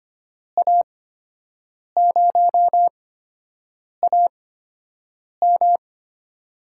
Morse code, heard as a keyed tone, about 25 wpm, spaced as for 7 wpm, 700 Hz. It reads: A0AM
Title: USA Call Signs - 5 Characters - 8x Character Spacing - 25wpm